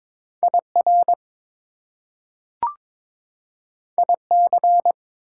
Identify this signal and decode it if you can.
Morse code, heard as a keyed tone, about 21 wpm, 700 Hz. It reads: IR IC
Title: Prefixes and Suffixes 22wpm